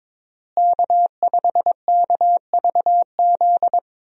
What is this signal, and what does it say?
Morse code, heard as a keyed tone, about 21 wpm, 700 Hz. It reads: K5KVZ
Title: USA Call Signs - 5 Characters - Repeated 3x - 22wpm